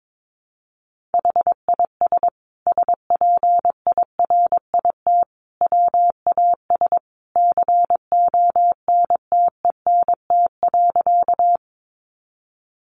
Morse code, heard as a keyed tone, about 22 wpm, 700 Hz. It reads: HIS SPIRIT WAS CONTENT.
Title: Sentences from Top 1400 Words 22wpm